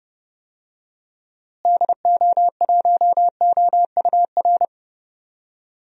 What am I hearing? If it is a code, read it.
Morse code, heard as a keyed tone, about 30 wpm, 700 Hz. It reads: DO1OUR
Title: DXCC-100 Call Signs - 6 Characters - Repeated 2x - 30wpm